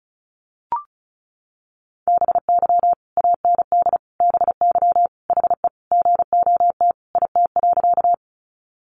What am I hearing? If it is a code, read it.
Morse code, heard as a keyed tone, about 34 wpm, 700 Hz. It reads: BY AND BY HE GOT IT.
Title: Sentences from Top 300 Words 35wpm